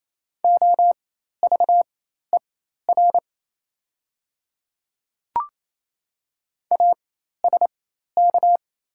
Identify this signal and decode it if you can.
Morse code, heard as a keyed tone, about 28 wpm, spaced as for 13 wpm, 700 Hz. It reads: OVER ASK
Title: 4x Character Spacing - Top 200 Words - 28wpm